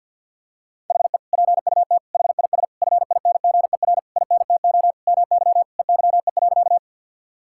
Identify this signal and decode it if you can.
Morse code, heard as a keyed tone, about 50 wpm, 700 Hz. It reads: HE PUT HIS FINGER INTO MY EYE.
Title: Sentences from Top 1600 Words 50wpm